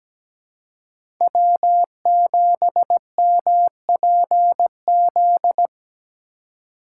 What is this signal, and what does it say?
Morse code, heard as a keyed tone, about 17 wpm, 700 Hz. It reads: W7MPZ
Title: USA Call Signs - 5 Characters - 4x Character Spacing - 17wpm